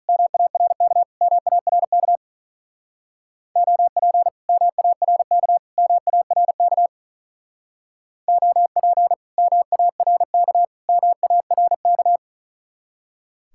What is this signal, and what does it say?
Morse code, heard as a keyed tone, about 43 wpm, 700 Hz. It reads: MARK MARK OP MARK MARK OP MARK MARK
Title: Speed-Racing Common QSO Phrases - 35wpm